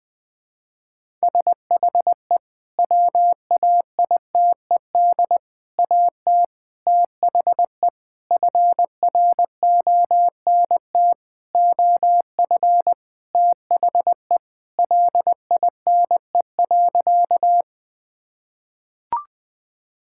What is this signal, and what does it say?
Morse code, heard as a keyed tone, about 20 wpm, 700 Hz. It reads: SHE WAITED AT THE FRONT OF THE LINE.
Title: Sentences with 5-Word Phrases - 20wpm